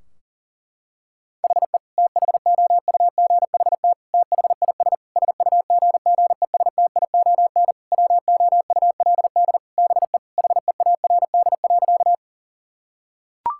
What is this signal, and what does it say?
Morse code, heard as a keyed tone, about 37 wpm, 700 Hz. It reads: HE THOUGHT THIS SUGGESTION WOULD BE HEARD.
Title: Sentences from Top 1600 Words 40wpm